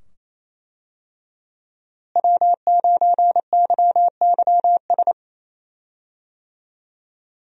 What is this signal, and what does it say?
Morse code, heard as a keyed tone, about 28 wpm, 700 Hz. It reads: W9YYS
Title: USA Call Signs - 5 Characters - Repeated 2x - 28wpm